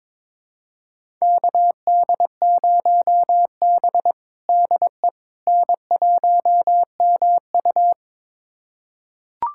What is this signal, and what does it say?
Morse code, heard as a keyed tone, about 22 wpm, 700 Hz. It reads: KD0B DE N1MU
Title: USA Callsign DE Callsign - 4 Characters - 22wpm